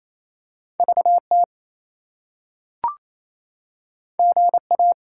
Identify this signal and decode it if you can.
Morse code, heard as a keyed tone, about 28 wpm, 700 Hz. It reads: VT GA
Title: US States and Canadian Provinces 28wpm